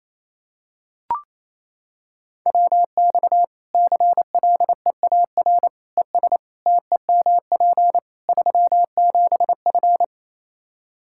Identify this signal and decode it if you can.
Morse code, heard as a keyed tone, about 27 wpm, 700 Hz. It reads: WX CLEAR ES TEMP 37F
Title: K1USN SST QSO Practice 28wpm